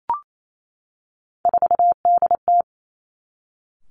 Morse code, heard as a keyed tone, about 28 wpm, 700 Hz. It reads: 4DT
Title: Three Challenging Letter-Number Combinations 28wpm